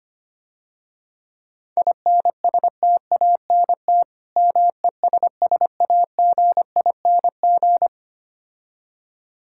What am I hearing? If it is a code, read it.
Morse code, heard as a keyed tone, about 25 wpm, 700 Hz. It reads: INSTANT MESSAGING
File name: Easy Words - Speed-Racing with Hints - 25wpm